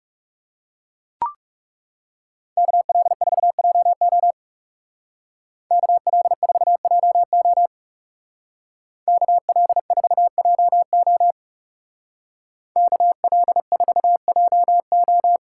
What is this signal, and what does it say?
Morse code, heard as a keyed tone, about 36 wpm, 700 Hz. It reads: KL4JO KL4JO KL4JO KL4JO
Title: Speed Racing - USA 2x2 Call Signs 30wpm